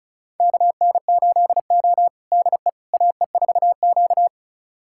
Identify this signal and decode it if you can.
Morse code, heard as a keyed tone, about 34 wpm, 700 Hz. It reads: KN8O DE AE4Q
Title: USA Callsign DE Callsign - 4 Characters - 35wpm